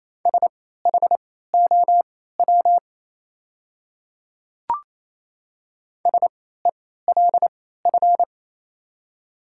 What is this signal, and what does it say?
Morse code, heard as a keyed tone, about 27 wpm, spaced as for 16 wpm, 690 Hz. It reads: SHOW SELF
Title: 3x Character Spacing - Top 200 Words - 28wpm